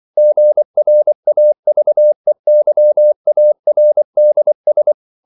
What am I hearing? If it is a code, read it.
Morse code, heard as a keyed tone, about 24 wpm, 600 Hz. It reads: GRAVEYARDS
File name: CW / Practice / EnglishQuotations0324